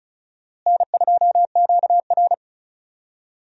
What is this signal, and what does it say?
Morse code, heard as a keyed tone, about 36 wpm, 700 Hz. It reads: N2QR
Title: USA Call Signs - 4 Characters - Repeated 3x - 35wpm